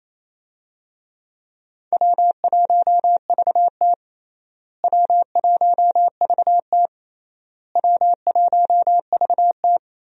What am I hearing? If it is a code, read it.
Morse code, heard as a keyed tone, about 28 wpm, 700 Hz. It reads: W1VT W1VT W1VT
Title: USA Call Signs - 4 Characters - Repeated 3x - 28wpm